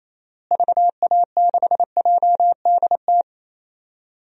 Morse code, heard as a keyed tone, about 28 wpm, 700 Hz. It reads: VA6JDT
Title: DXCC-100 Call Signs - 6 Characters - Repeated 3x - 28wpm